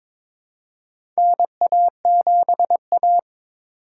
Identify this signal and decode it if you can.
Morse code, heard as a keyed tone, about 22 wpm, 700 Hz. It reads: NA7A
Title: USA Call Signs - 4 Characters - 3x Character Spacing - 22wpm